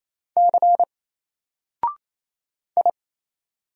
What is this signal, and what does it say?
Morse code, heard as a keyed tone, about 27 wpm, 700 Hz. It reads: C I
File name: Single-Letter-Number-28wpm